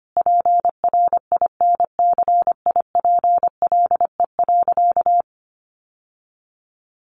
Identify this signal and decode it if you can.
Morse code, heard as a keyed tone, about 25 wpm, 700 Hz. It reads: PRINCIPLE.
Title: Sentences from Top 1400 Words 25wpm